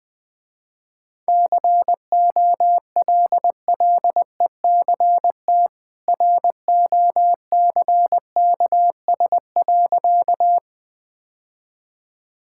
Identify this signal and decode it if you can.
Morse code, heard as a keyed tone, about 20 wpm, 700 Hz. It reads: COLLECT ROCKS.
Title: Easy Sentences with Hints - 20wpm